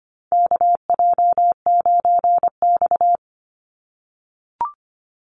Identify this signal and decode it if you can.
Morse code, heard as a keyed tone, about 25 wpm, 700 Hz. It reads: KJ9X
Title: NA SOTA Chaser Callsigns 25wpm